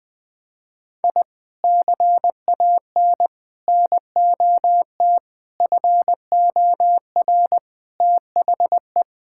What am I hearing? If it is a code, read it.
Morse code, heard as a keyed tone, about 20 wpm, 700 Hz. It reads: I CAN NOT FOR THE
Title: Sets of 8 Words - 20wpm